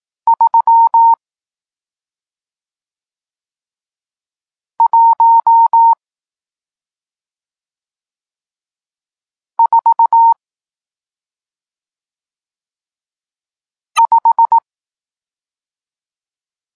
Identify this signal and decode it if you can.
Morse code, heard as a keyed tone, about 18 wpm, 920 Hz. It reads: 3 1 4 5
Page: Tekst 40 – Nauka znaków przestankowych